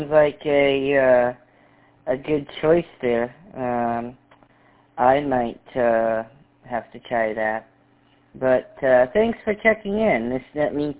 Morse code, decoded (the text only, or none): none